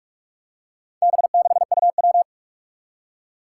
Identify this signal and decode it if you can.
Morse code, heard as a keyed tone, about 44 wpm, 700 Hz. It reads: B6UW
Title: Four Challenging Letter-Number Combinations 45wpm